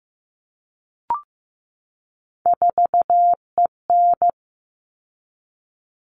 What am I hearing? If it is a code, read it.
Morse code, heard as a keyed tone, about 15 wpm, 700 Hz. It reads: 4EN
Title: All Cut Numbers - 3-Digit 15wpm